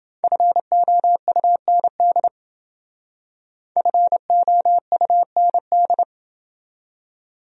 Morse code, heard as a keyed tone, about 28 wpm, 700 Hz. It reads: FOUND FOUND